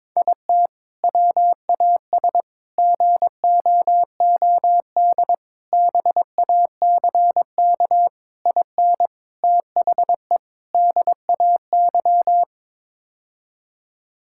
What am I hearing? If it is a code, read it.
Morse code, heard as a keyed tone, about 21 wpm, 700 Hz. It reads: IT WAS GOOD BACK IN THE DAY
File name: practice / 22wpm / Sentences with 3-Word Phrases - 22wpm